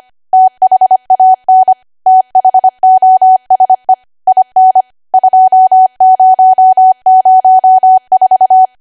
Morse code, heard as a keyed tone, about 25 wpm, 750 Hz. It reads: THAN THOSE IN 2004